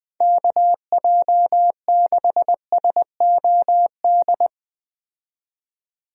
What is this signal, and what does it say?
Morse code, heard as a keyed tone, about 20 wpm, 700 Hz. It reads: KJ6SOD
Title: USA Call Signs - 6 Characters - 20wpm